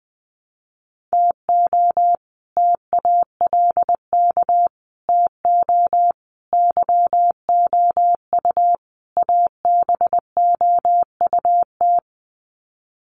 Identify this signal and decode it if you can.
Morse code, heard as a keyed tone, about 20 wpm, 700 Hz. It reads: TO TALK TO YOU ABOUT
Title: Sets of 5 Words Encore 20wpm